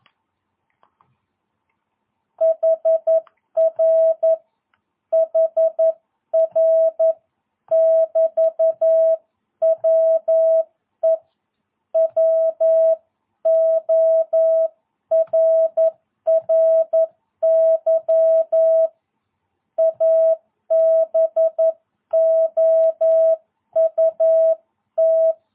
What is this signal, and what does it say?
Morse code, heard as a keyed tone, about 11 wpm, 660 Hz. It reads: HR HR <BT> WE WORRY ABOUT
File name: jh8cbh / 2022cw